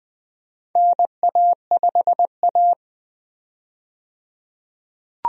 Morse code, heard as a keyed tone, about 20 wpm, 700 Hz. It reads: NA5A
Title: USA Call Signs - 4 Characters - 20wpm